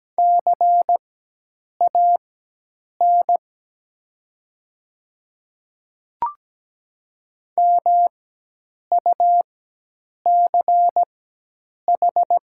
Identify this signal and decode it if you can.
Morse code, heard as a keyed tone, about 17 wpm, spaced as for 8 wpm, 700 Hz. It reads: CAN MUCH
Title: 4x Character Spacing - Top 300 Words - 17wpm